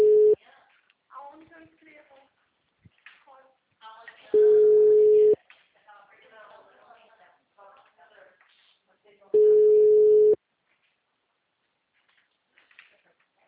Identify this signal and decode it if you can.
Ringback tone (Europe, 425 Hz), with speech.